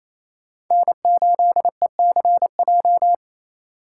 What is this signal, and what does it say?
Morse code, heard as a keyed tone, about 36 wpm, 700 Hz. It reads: N8ECJ